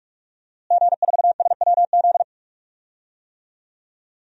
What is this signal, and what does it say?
Morse code, heard as a keyed tone, about 48 wpm, 700 Hz. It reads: G4SWZ
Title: DXCC-100 Call Signs - 5 Characters - Repeated 3x - 45wpm